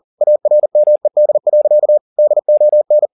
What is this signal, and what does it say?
Morse code, heard as a keyed tone, about 40 wpm, 600 Hz. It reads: ARMED. DON
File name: the old man and the sea 2 40wpm